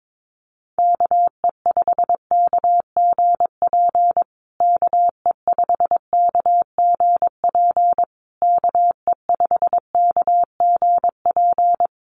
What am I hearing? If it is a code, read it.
Morse code, heard as a keyed tone, about 22 wpm, 700 Hz. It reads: KE5KGP KE5KGP KE5KGP
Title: USA Call Signs Repeated 3x 22wpm